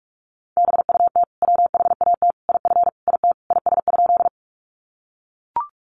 Morse code, heard as a keyed tone, about 44 wpm, 700 Hz. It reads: BUT WHAT IF IT IS?
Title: Sentences from Top 100 Words 45wpm